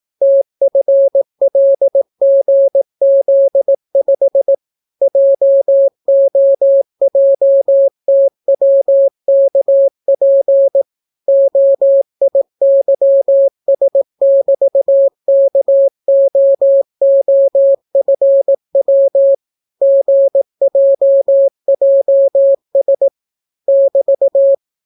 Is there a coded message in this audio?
Yes: Morse code from 0.2 to 24.5 s, heard as a keyed tone, about 18 wpm, 550 Hz. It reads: TFLGZ5 JOJTWKP OIYS<BT>KOOFW GJJS <BT>